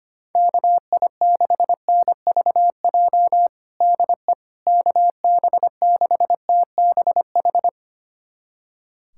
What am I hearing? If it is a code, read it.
Morse code, heard as a keyed tone, about 25 wpm, 700 Hz. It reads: KI6NVJ DE KB6TBH
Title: USA Callsign DE Callsign - 6 Characters - 25wpm